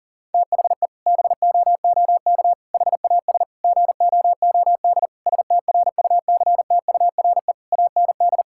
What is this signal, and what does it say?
Morse code, heard as a keyed tone, about 40 wpm, 700 Hz. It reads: THE BOOK HAS GOOD STRUCTURE AND
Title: Sentences from Top 1000 Words - Review Part 6 - 40wpm